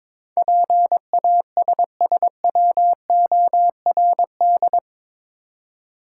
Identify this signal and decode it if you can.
Morse code, heard as a keyed tone, about 22 wpm, 700 Hz. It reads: PASSWORD
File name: practice / 22wpm / Eight Letter Words - 22wpm